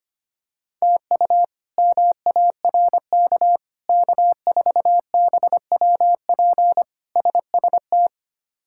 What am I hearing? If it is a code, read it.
Morse code, heard as a keyed tone, about 25 wpm, 700 Hz. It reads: TU MARK K4BWP SST
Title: K1USN SST QSO Practice - 25wpm